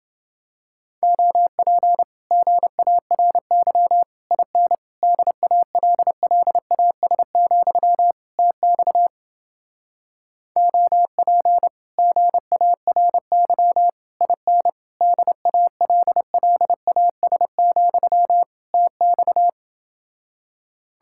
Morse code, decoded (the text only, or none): OP GARY IN DALLAS, TX OP GARY IN DALLAS, TX